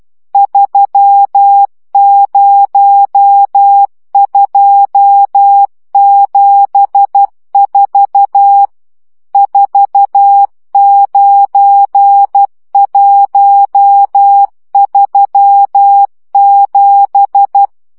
Morse code, heard as a keyed tone, about 12 wpm, 800 Hz. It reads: 30274 49137
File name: morze vizsga 2011-2 10